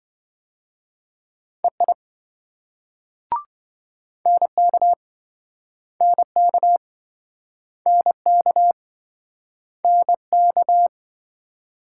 Morse code, heard as a keyed tone, about 27 wpm, 700 Hz. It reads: EI NK NK NK NK